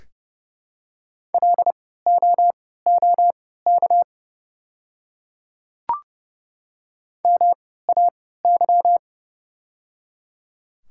Morse code, heard as a keyed tone, about 30 wpm, spaced as for 17 wpm, 700 Hz. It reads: LOOK MAY
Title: 3x Character Spacing - Top 500 Words - 30wpm